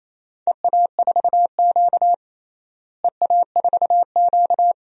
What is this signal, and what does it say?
Morse code, heard as a keyed tone, about 28 wpm, 700 Hz. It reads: EA4Q EA4Q
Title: DXCC-100 Call Signs - 4 Characters - Repeated 2x - 28wpm